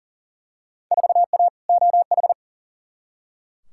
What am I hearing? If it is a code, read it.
Morse code, heard as a keyed tone, about 40 wpm, 700 Hz. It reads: 4A OH
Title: Field Day Exchanges - 40wpm